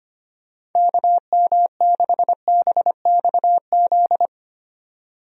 Morse code, heard as a keyed tone, about 25 wpm, 700 Hz. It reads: KM6BXZ
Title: USA Call Signs - 6 Characters - 25wpm